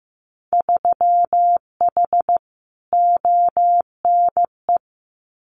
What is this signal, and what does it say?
Morse code, heard as a keyed tone, about 15 wpm, 700 Hz. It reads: 3H ONE
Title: Winter Field Day Exchanges Encore 15wpm